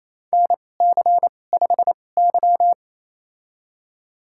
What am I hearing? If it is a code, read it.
Morse code, heard as a keyed tone, about 28 wpm, spaced as for 20 wpm, 700 Hz. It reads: NC5Y
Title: USA Call Signs - 2x Character Spacing - 28wpm